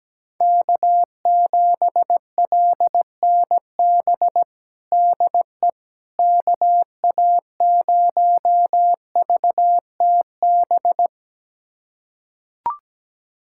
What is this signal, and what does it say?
Morse code, heard as a keyed tone, about 17 wpm, 700 Hz. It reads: K7LNB DE KA0VTB
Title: USA Callsign DE Callsign 17wpm